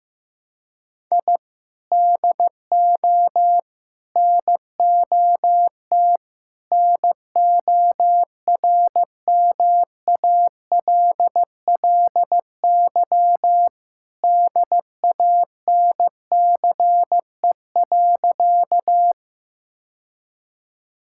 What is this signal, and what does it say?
Morse code, heard as a keyed tone, about 15 wpm, 700 Hz. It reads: I DO NOT NORMALLY DANCE.
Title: Sentences from Top 1000 Words - Review Part 6 - 15wpm